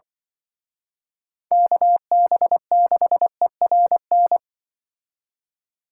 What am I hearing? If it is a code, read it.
Morse code, heard as a keyed tone, about 25 wpm, 700 Hz. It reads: KB6ERN